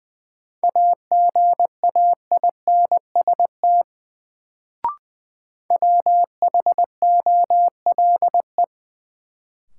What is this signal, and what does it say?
Morse code, heard as a keyed tone, about 21 wpm, 700 Hz. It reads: AGAINST WHOLE
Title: Sets of 2 Words - New Word Drill - 20wpm